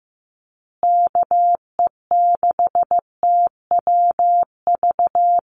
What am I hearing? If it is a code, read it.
Morse code, heard as a keyed tone, about 15 wpm, 700 Hz. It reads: KE6TWV
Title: USA Call Signs Repeated 2x 15wpm